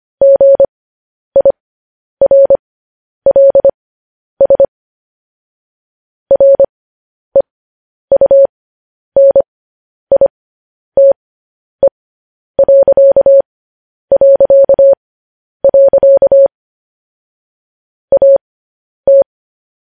Morse code, heard as a keyed tone, about 25 wpm, spaced as for 10 wpm, 550 Hz. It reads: GIRLS REUNITE... AT
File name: podcast / News headlines-1x-25x10